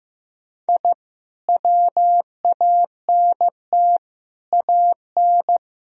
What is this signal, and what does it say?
Morse code, heard as a keyed tone, about 15 wpm, 700 Hz. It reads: I WANT AN
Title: Sentences from Top 1000 Words - Review Part 1 - 15wpm